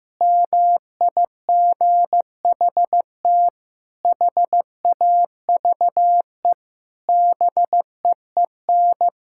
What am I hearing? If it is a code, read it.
Morse code, heard as a keyed tone, about 16 wpm, 700 Hz. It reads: MIGHT HAVE BEEN
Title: Sentences from Top 1000 Words - Review Part 5 - 15wpm